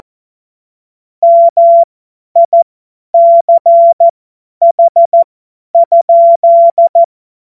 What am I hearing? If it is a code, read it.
Morse code, heard as a keyed tone, about 14 wpm, spaced as for 10 wpm, 680 Hz. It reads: MICH?
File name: Morsekurs / Witze / Witz 09. Februar 2021, 50 F70